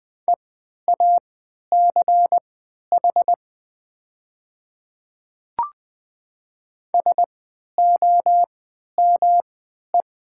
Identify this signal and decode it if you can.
Morse code, heard as a keyed tone, about 20 wpm, spaced as for 11 wpm, 700 Hz. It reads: EACH SOME
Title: 3x Character Spacing - Top 100 Words - 20wpm